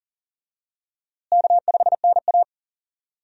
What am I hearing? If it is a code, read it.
Morse code, heard as a keyed tone, about 40 wpm, 700 Hz. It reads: K5NA
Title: USA Call Signs - 4 Characters - Repeated 3x - 40wpm